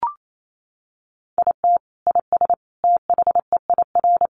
Morse code, heard as a keyed tone, about 28 wpm, 700 Hz. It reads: IT IS THEIR
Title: Sentences from Top 1000 Words 28wpm